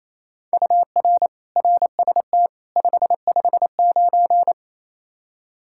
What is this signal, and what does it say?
Morse code, heard as a keyed tone, about 28 wpm, 700 Hz. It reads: UR RST 559
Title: K1USN SST QSO Practice 28wpm